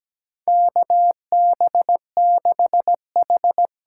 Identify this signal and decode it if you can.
Morse code, heard as a keyed tone, about 17 wpm, 700 Hz. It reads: KB6H
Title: USA Call Signs - 6 Characters - 8x Character Spacing - 17wpm